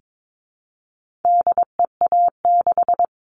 Morse code, heard as a keyed tone, about 22 wpm, 700 Hz. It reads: DEA6
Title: All Cut Numbers - 4-Digit 22wpm